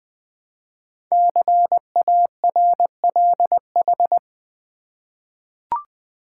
Morse code, heard as a keyed tone, about 20 wpm, 700 Hz. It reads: CARLH
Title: Code Groups - 5 Letters - 20wpm